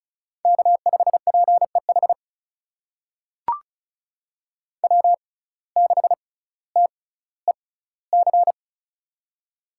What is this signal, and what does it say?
Morse code, heard as a keyed tone, about 36 wpm, spaced as for 18 wpm, 700 Hz. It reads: K5PEH W6TEC
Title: USA Call Signs - 5 Characters - 6x Character Spacing - 35wpm